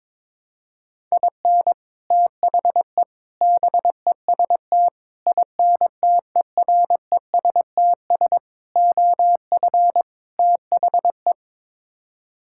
Morse code, heard as a keyed tone, about 22 wpm, 700 Hz. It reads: IN THE BEST INTERESTS OF THE
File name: practice / 22wpm / Sets of 6 Words - 22wpm